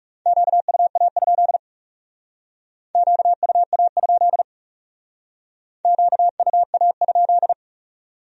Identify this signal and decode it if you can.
Morse code, heard as a keyed tone, about 36 wpm, 700 Hz. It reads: QUA? QUA? QUA?
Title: Speed-Racing Q-Codes - 30wpm